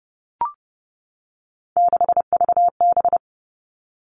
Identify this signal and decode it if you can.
Morse code, heard as a keyed tone, about 30 wpm, 700 Hz. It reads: 6VB
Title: Three Challenging Letter-Number Combinations 30wpm